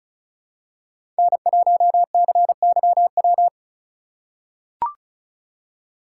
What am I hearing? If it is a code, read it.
Morse code, heard as a keyed tone, about 36 wpm, 700 Hz. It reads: N1CYW
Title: USA Call Signs - 5 Characters - 2x Character Spacing - 35wpm